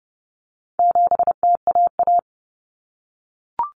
Morse code, heard as a keyed tone, about 30 wpm, 700 Hz. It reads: 7TAA
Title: Common Cut Numbers - 4-Digit 30wpm